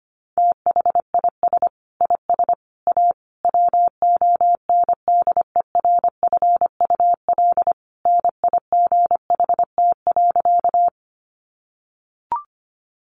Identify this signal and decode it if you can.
Morse code, heard as a keyed tone, about 25 wpm, 700 Hz. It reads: THIS IS A WONDERFUL NIGHT.
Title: Sentences from Top 1100 Words 25wpm